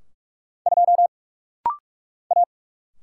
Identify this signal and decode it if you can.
Morse code, heard as a keyed tone, about 44 wpm, 700 Hz. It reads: 2 A
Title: Single Letter-Number - Rapid-Fire 45wpm